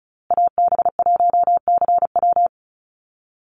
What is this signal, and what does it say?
Morse code, heard as a keyed tone, about 34 wpm, 700 Hz. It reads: AB1CW
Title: Call Signs Repeated 2x - 35wpm